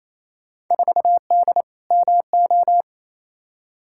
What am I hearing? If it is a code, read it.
Morse code, heard as a keyed tone, about 28 wpm, 700 Hz. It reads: 4D MO
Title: Field Day Exchanges Encore - 28wpm